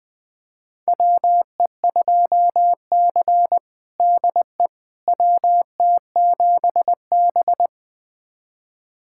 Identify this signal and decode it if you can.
Morse code, heard as a keyed tone, about 21 wpm, 700 Hz. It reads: WE2C DE WT7B
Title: USA Callsign DE Callsign - 4 Characters - 20wpm